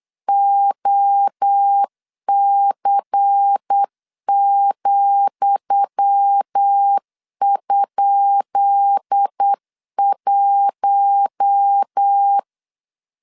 Morse code, heard as a keyed tone, about 8 wpm, 800 Hz. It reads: OC,?1
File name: LEZIONI / 10 SIGNS / 10D COMA 8wpm 3749